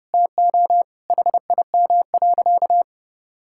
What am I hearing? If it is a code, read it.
Morse code, heard as a keyed tone, about 30 wpm, 700 Hz. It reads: TO HIM.
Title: Sentences from Top 1000 Words - Review Part 6 - 30wpm